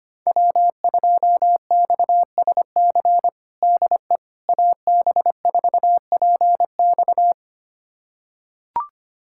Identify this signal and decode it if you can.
Morse code, heard as a keyed tone, about 25 wpm, 700 Hz. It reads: W2XSC DE AB4PX
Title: USA Callsign DE Callsign - 5 Characters - 25wpm